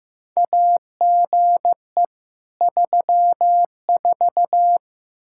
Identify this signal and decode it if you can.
Morse code, heard as a keyed tone, about 15 wpm, 700 Hz. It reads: AGE 34
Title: Practice QSO - Exchange 3 - More Info - 15wpm